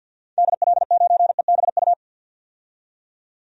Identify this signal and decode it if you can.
Morse code, heard as a keyed tone, about 52 wpm, 700 Hz. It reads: DL9EBU
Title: DXCC-100 Call Signs - 6 Characters - 50wpm